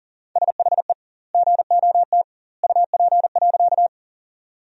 Morse code, heard as a keyed tone, about 40 wpm, 700 Hz. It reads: SHE GOT UP.
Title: 2x Word-Spacing - Sentences from Top 300 Words - 40wpm